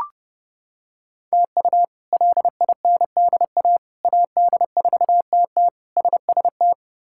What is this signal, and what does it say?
Morse code, heard as a keyed tone, about 30 wpm, 700 Hz. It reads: TU LINDA AD4TT SST
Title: K1USN SST QSO Practice - 30wpm